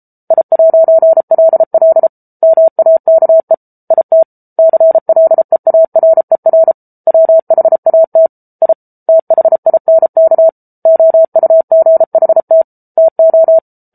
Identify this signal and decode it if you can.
Morse code, heard as a keyed tone, about 34 wpm, 650 Hz. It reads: I'LL MAKE IT CLEARER WHAT I THINK OUGHT TO